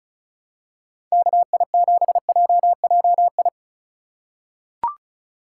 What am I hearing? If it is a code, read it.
Morse code, heard as a keyed tone, about 36 wpm, 700 Hz. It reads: KI7JJI
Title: USA Call Signs - 6 Characters - 3x Character Spacing - 35wpm